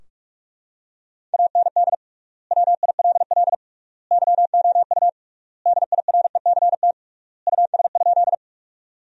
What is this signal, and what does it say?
Morse code, heard as a keyed tone, about 42 wpm, 700 Hz. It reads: AND WILL YOU DIRECT US?
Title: 3x Word-Spacing - Sentences from Top 400 Words - 45wpm